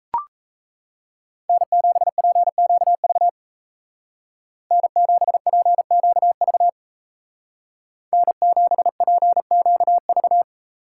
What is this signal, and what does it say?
Morse code, heard as a keyed tone, about 34 wpm, 700 Hz. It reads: N7PQV N7PQV N7PQV
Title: Speed Racing - USA Call Signs 28wpm